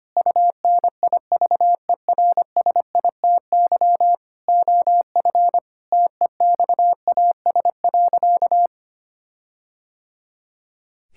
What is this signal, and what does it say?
Morse code, heard as a keyed tone, about 25 wpm, 700 Hz. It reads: UNIVERSITY OF TEXAS.
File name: practice / 25wpm / Sentences with 4-Word Phrases - 25wpm